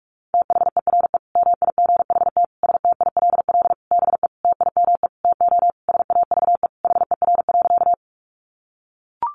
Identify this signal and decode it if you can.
Morse code, heard as a keyed tone, about 44 wpm, 700 Hz. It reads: THERE MIGHT STILL BE TIME TO SAVE HER.
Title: Sentences from Top 1100 Words 45wpm